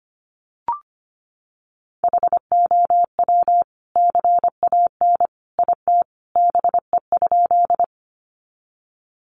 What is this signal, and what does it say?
Morse code, heard as a keyed tone, about 25 wpm, 700 Hz. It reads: HOW CAN IT BE?
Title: Sentences from Top 100 Words 25wpm